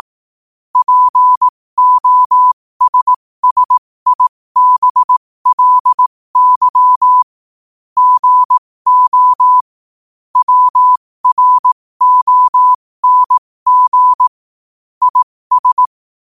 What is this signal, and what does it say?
Morse code, heard as a keyed tone, about 18 wpm, spaced as for 15 wpm, 1000 Hz. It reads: POSSIBLY GO WRONG IS
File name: qotdfiles / 15wpm